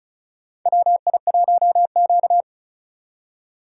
Morse code, heard as a keyed tone, about 36 wpm, 700 Hz. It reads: WI1Q
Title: USA Call Signs - 4 Characters - 3x Character Spacing - 35wpm